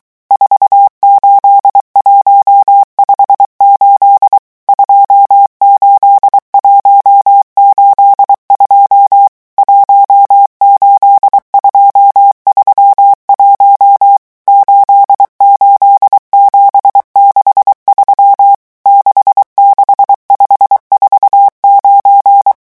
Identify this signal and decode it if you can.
Morse code, heard as a keyed tone, about 23 wpm, 800 Hz. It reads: 48158 28182 18231 88763 66549
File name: morse mp3 / revision11 chiffres vitesse 22